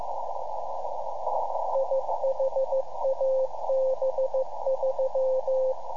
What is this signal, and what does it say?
Morse code, heard as a keyed tone, about 15 wpm, 530 Hz, with radio noise. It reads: IHAB3
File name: iHAB-3 / Repeater Recordings